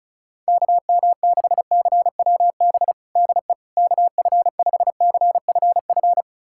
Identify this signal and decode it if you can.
Morse code, heard as a keyed tone, about 36 wpm, 700 Hz. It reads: KM6CWB DE KF5CFF
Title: USA Callsign DE Callsign 35wpm